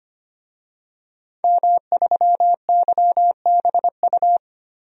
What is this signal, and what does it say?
Morse code, heard as a keyed tone, about 25 wpm, 700 Hz. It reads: M3YBU
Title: DXCC-100 Call Signs - 5 Characters - 25wpm